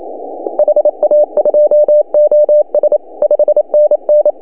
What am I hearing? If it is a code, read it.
Morse code, heard as a keyed tone, about 28 wpm, 590 Hz. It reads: HA2OS 5NN
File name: CW / HA2OS RHOB 2017 CW